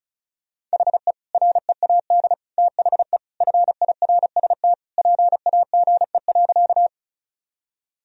Morse code, heard as a keyed tone, about 36 wpm, 700 Hz. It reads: HE READ THE FIRST PAGE.
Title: Sentences from Top 1000 Words - Review Part 1 - 35wpm